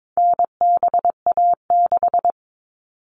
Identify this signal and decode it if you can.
Morse code, heard as a keyed tone, about 22 wpm, 700 Hz. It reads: NBA6
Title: All Cut Numbers - 4-Digit 22wpm